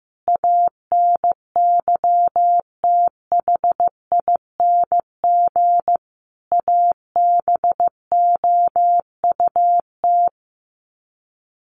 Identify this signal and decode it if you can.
Morse code, heard as a keyed tone, about 15 wpm, 700 Hz. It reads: ANYTHING ABOUT
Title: Sets of 5 Words 15wpm